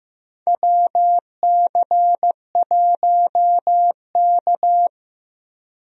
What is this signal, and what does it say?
Morse code, heard as a keyed tone, about 15 wpm, 700 Hz. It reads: WC1K
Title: USA Call Signs - 4 Characters - Repeated 2x - 15wpm